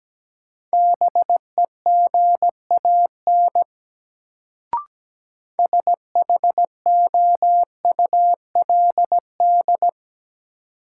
Morse code, heard as a keyed tone, about 17 wpm, 700 Hz. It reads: BEGAN SHOULD
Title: Sentences from Top 300 Words - New Word Drill - 17wpm